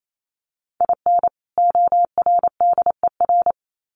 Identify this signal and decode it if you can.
Morse code, heard as a keyed tone, about 28 wpm, 700 Hz. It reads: IN ORDER
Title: Sets of 2 Words 28wpm